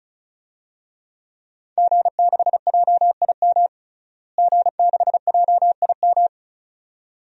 Morse code, heard as a keyed tone, about 36 wpm, 700 Hz. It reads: G6JIM G6JIM
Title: DXCC-100 Call Signs - 5 Characters - Repeated 2x - 35wpm